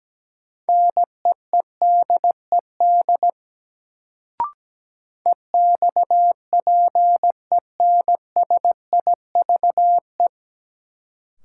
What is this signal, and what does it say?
Morse code, heard as a keyed tone, about 17 wpm, 700 Hz. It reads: NEEDED EXPENSIVE
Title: Sentences from Top 1000 Words - Review Part 6 - New Word Drill - 17wpm